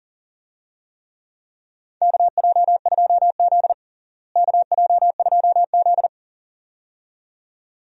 Morse code, heard as a keyed tone, about 40 wpm, 700 Hz. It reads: KJ2Z KJ2Z
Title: USA Call Signs - 4 Characters - Repeated 2x - 40wpm